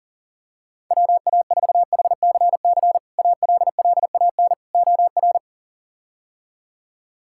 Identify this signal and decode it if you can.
Morse code, heard as a keyed tone, about 40 wpm, 700 Hz. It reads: WA4HCC ALLAN OR
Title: K1USN SST QSO Practice - 40wpm